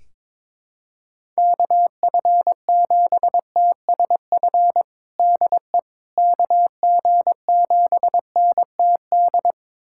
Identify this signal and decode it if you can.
Morse code, heard as a keyed tone, about 22 wpm, 700 Hz. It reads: KF7TSF DE KG7NTD
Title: USA Callsign DE Callsign - 6 Characters - 22wpm